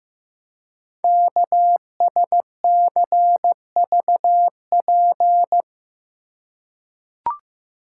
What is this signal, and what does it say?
Morse code, heard as a keyed tone, about 15 wpm, 700 Hz. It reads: KSCVP
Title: Code Groups - 5 Letters - 15wpm